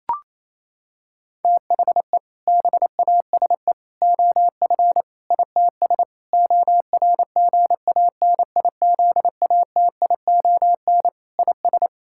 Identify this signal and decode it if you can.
Morse code, heard as a keyed tone, about 28 wpm, 700 Hz. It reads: THE BASE OF ITS ORGANIZATION IS
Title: Sentences from Top 1000 Words - Review Part 5 - 28wpm